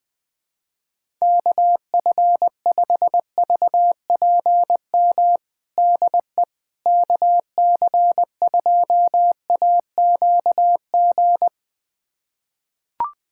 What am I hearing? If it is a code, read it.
Morse code, heard as a keyed tone, about 20 wpm, 700 Hz. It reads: KF5VPM DE KC2AQG
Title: USA Callsign DE Callsign - 6 Characters - 20wpm